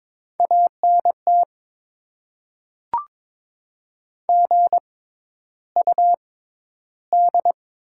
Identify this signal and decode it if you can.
Morse code, heard as a keyed tone, about 22 wpm, spaced as for 11 wpm, 690 Hz. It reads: ANT GUD
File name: Top 20 QSO Elements - 6x Character Spacing - 22wpm